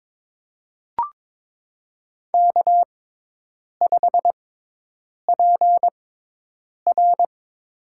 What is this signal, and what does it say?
Morse code, heard as a keyed tone, about 22 wpm, spaced as for 8 wpm, 700 Hz. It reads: K5PR
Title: USA Call Signs - 4 Characters - 6x Character Spacing - 22wpm